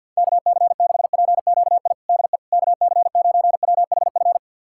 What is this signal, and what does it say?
Morse code, heard as a keyed tone, about 52 wpm, 700 Hz. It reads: KC6PQI DE KC9WHF
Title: USA Callsign DE Callsign - 6 Characters - 50wpm